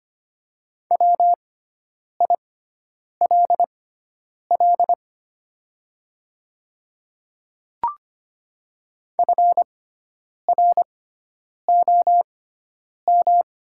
Morse code, heard as a keyed tone, about 25 wpm, spaced as for 9 wpm, 700 Hz. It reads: WILL FROM
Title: 6x Character Spacing - Top 100 Words - 25wpm